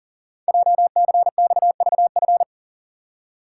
Morse code, heard as a keyed tone, about 42 wpm, 690 Hz. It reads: JCXVF
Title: Code Groups - 5 Letters - 40wpm